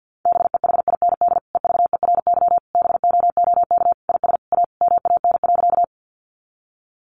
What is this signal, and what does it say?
Morse code, heard as a keyed tone, about 50 wpm, 700 Hz. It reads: BEHIND EVERY BOOK IS A MAN.
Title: Sentences from Top 500 Words 50wpm